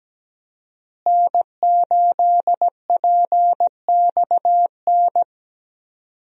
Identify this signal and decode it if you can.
Morse code, heard as a keyed tone, about 17 wpm, 700 Hz. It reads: N8PXN
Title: USA Call Signs - 5 Characters - Repeated 3x - 17wpm